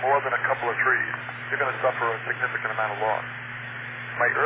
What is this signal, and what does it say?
Emergency Alert System (SAME) tones, heard as harsh buzzing bursts, with speech. It is heard from 1.2 to 3.6 s.